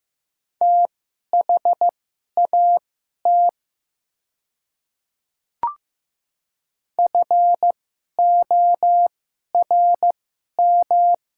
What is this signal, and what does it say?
Morse code, heard as a keyed tone, about 15 wpm, spaced as for 11 wpm, 700 Hz. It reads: THAT FORM
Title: 2x Character Spacing - Top 200 Words - 15wpm